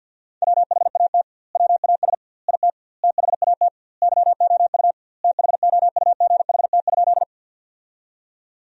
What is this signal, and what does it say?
Morse code, heard as a keyed tone, about 50 wpm, 700 Hz. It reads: WHAT WAS IT THAT YOU THOUGHT?
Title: 2x Word-Spacing - Sentences from Top 300 Words - 50wpm